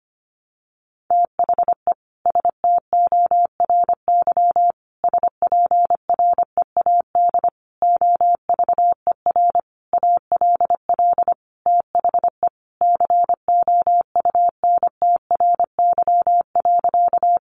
Morse code, heard as a keyed tone, about 25 wpm, 700 Hz. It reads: THE STORY SPREAD OVER ALL THE COUNTRY.
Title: Sentences from Top 1200 Words 25wpm